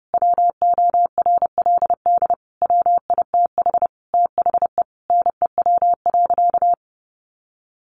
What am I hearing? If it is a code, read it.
Morse code, heard as a keyed tone, about 28 wpm, 700 Hz. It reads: WORLD WITH THE NEW.
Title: Sentences from Top 1300 Words 30wpm